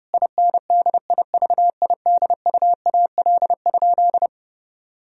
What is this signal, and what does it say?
Morse code, heard as a keyed tone, about 30 wpm, 700 Hz. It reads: INDIVIDUAL?
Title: Sentences from Top 1000 Words - Review Part 4 - 30wpm